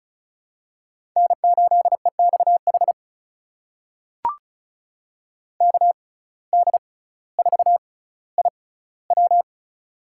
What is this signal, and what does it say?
Morse code, heard as a keyed tone, about 34 wpm, spaced as for 18 wpm, 700 Hz. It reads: N8EXH KD4IW
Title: USA Call Signs - 5 Characters - 6x Character Spacing - 35wpm